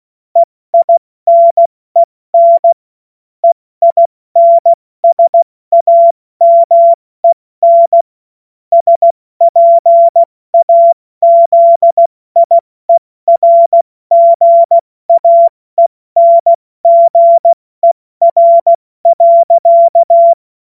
Morse code, heard as a keyed tone, about 16 wpm, spaced as for 14 wpm, 680 Hz. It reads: EINEN EINSAMEN SPAZIERGAENGER.
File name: Morsekurs / Witze / Witz 16. August 70 F80